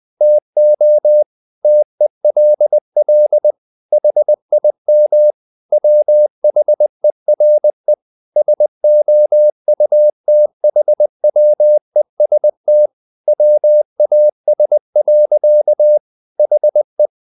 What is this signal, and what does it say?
Morse code, heard as a keyed tone, about 20 wpm, 600 Hz. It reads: TO TELL HIM WHERE SOUTHWEST WAS. HE
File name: the old man and the sea part 2 20wpm